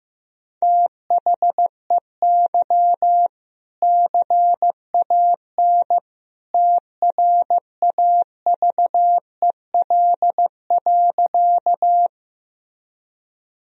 Morse code, heard as a keyed tone, about 15 wpm, 700 Hz. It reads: THEY CAN TRAVEL.
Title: Sentences from Top 1000 Words - Review Part 2 - 15wpm